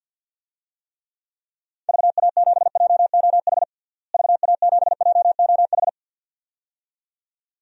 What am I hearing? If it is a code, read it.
Morse code, heard as a keyed tone, about 47 wpm, 700 Hz. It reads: VA7JOH VA7JOH
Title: DXCC-100 Call Signs - 6 Characters - Repeated 2x - 50wpm